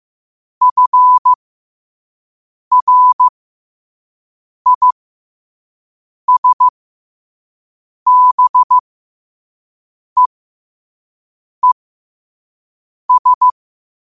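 Morse code, heard as a keyed tone, about 15 wpm, spaced as for 5 wpm, 1000 Hz. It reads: FRISBEES
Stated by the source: Morse code quotes sent at 5 words per minute.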